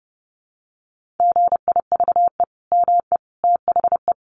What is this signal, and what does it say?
Morse code, heard as a keyed tone, about 30 wpm, 700 Hz. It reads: GIVE ME THE
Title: Sentences from Top 1400 Words 30wpm